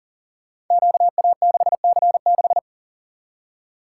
Morse code, heard as a keyed tone, about 40 wpm, 700 Hz. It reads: QA6C6
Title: Code Groups - 5 Letter:Numbers - 40wpm